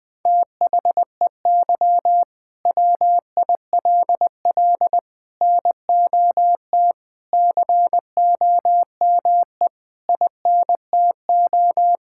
Morse code, heard as a keyed tone, about 20 wpm, 700 Hz. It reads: THEY WILL NOT COME INTO
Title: Sentences from Top 1000 Words - Review Part 5 - 20wpm